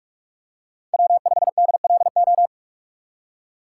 Morse code, heard as a keyed tone, about 46 wpm, 700 Hz. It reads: W5DLO
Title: USA Call Signs - 8x Character Spacing - 45wpm